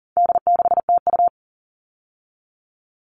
Morse code, heard as a keyed tone, about 40 wpm, 700 Hz. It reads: D6TU
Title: All Cut Numbers - 4-Digit 40wpm